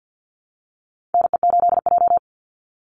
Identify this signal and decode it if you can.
Morse code, heard as a keyed tone, about 50 wpm, 700 Hz. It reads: NE8J
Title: Call Signs - 50wpm